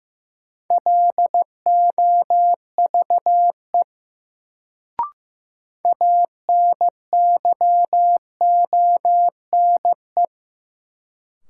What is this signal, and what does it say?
Morse code, heard as a keyed tone, about 15 wpm, 700 Hz. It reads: LOVE ANYONE
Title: Sets of 5 Words Encore - New Word Drill - 15wpm